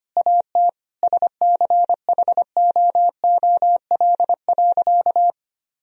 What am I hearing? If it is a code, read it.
Morse code, heard as a keyed tone, about 25 wpm, 700 Hz. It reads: AT SCHOOL.
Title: Sentences from Top 300 Words and Abbreviations 25wpm